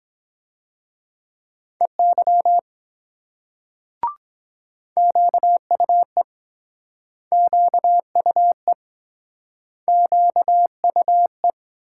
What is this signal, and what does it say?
Morse code, heard as a keyed tone, about 22 wpm, 700 Hz. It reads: EY QUE QUE QUE